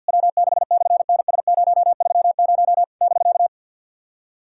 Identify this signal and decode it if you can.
Morse code, heard as a keyed tone, about 50 wpm, 700 Hz. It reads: W6/NS030 <BK>
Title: SOTA QSO Practice 50wpm